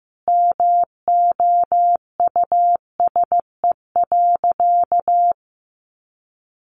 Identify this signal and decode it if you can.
Morse code, heard as a keyed tone, about 15 wpm, 700 Hz. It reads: MOUSE.
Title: Sentences from Top 700 Words 15wpm